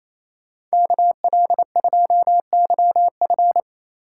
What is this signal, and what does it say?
Morse code, heard as a keyed tone, about 33 wpm, 700 Hz. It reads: KL2YF
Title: Speed Racing - USA 2x2 Call Signs 28wpm